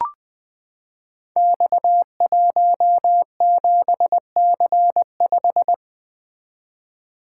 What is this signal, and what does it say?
Morse code, heard as a keyed tone, about 20 wpm, 700 Hz. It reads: X17C5
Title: Code Groups - 5 Letter:Numbers - 20wpm